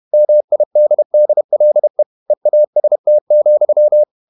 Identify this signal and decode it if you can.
Morse code, heard as a keyed tone, about 31 wpm, 600 Hz. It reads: MIDDLE EAST,
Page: Never 31wpm | CW med Gnister